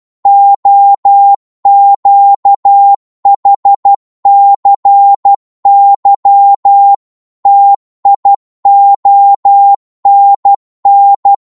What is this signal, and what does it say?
Morse code, heard as a keyed tone, about 12 wpm, 800 Hz. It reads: OQHCY TIONN
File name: Morse / ltr-12wpm